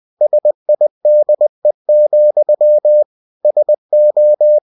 Morse code, heard as a keyed tone, about 20 wpm, 600 Hz. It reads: SIDE, SO
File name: cw-books / winnie-the-pooh / winnie-20wpm-page-0083